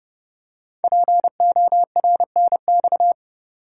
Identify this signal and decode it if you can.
Morse code, heard as a keyed tone, about 30 wpm, 700 Hz. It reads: PORNX
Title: Code Groups - 5 Letters - 30wpm